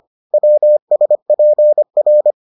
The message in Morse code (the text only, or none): WSPR